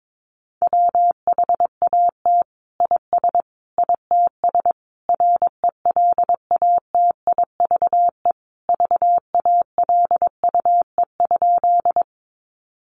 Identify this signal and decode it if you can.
Morse code, heard as a keyed tone, about 22 wpm, 700 Hz. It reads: WHAT IS ITS RELATIVE VALUE?
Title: Sentences from Top 1400 Words 22wpm